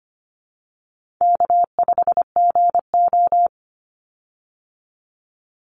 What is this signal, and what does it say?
Morse code, heard as a keyed tone, about 25 wpm, 700 Hz. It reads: K5GO
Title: Easy Call Signs Repeated 3x - 25wpm